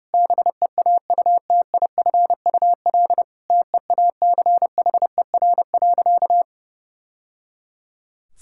Morse code, heard as a keyed tone, about 30 wpm, 700 Hz. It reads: BEAUTIFUL TEACHER.
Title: Sentences from Top 1000 Words - Review Part 6 - 30wpm